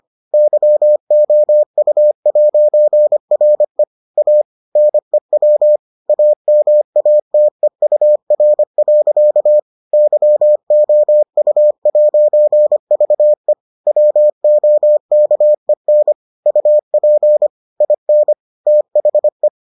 Morse code, heard as a keyed tone, about 25 wpm, 600 Hz. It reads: YOU'RE A NEW AMATEUR. YOU'VE WOKEN UP IN THE